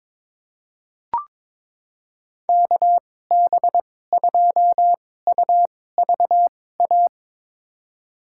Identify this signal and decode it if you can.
Morse code, heard as a keyed tone, about 22 wpm, spaced as for 16 wpm, 700 Hz. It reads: KB2UVA
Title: USA Call Signs - 2x Character Spacing - 22wpm